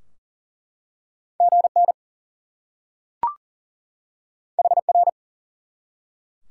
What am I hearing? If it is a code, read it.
Morse code, heard as a keyed tone, about 38 wpm, 700 Hz. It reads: GN HR
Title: Top 90 QSO Elements 40wpm